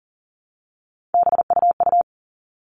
Morse code, heard as a keyed tone, about 40 wpm, 700 Hz. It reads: BUU
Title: All Cut Numbers - 3-Digit 40wpm